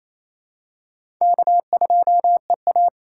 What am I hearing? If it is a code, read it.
Morse code, heard as a keyed tone, about 28 wpm, 700 Hz. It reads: K2EA
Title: USA Call Signs - 4 Characters - 2x Character Spacing - 28wpm